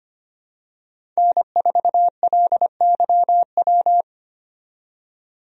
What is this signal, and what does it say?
Morse code, heard as a keyed tone, about 25 wpm, 700 Hz. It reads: N4LYW
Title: USA Call Signs - 5 Characters - 25wpm